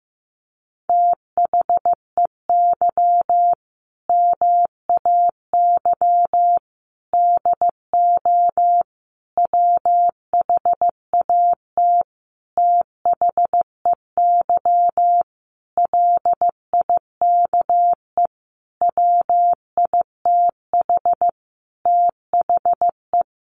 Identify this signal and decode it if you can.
Morse code, heard as a keyed tone, about 15 wpm, 700 Hz. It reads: THEY MAY DO WHAT THEY LIKE WITH THE
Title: Sentences from Top 1200 Words 15wpm